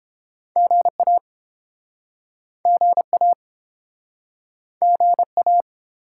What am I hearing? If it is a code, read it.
Morse code, heard as a keyed tone, about 32 wpm, 700 Hz. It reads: GA GA GA